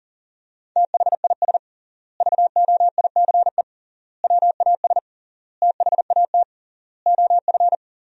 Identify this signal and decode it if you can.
Morse code, heard as a keyed tone, about 37 wpm, 700 Hz. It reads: THIS VOICE WAS THAT OF
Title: 3x Word-Spacing - Sentences from Top 500 Words - 40wpm